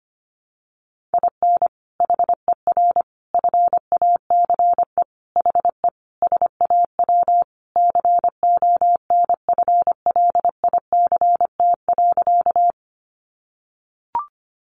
Morse code, heard as a keyed tone, about 25 wpm, 700 Hz. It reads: IN HER FACE HE SAW CONFLICT.
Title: Sentences from Top 1300 Words 25wpm